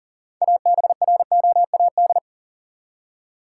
Morse code, heard as a keyed tone, about 40 wpm, 700 Hz. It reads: ABROAD
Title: Six Letter Words - 40wpm